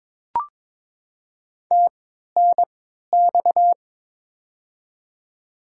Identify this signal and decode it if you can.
Morse code, heard as a keyed tone, about 22 wpm, spaced as for 13 wpm, 700 Hz. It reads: TNX
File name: Top 20 QSO Elements - 3x Character Spacing - 22wpm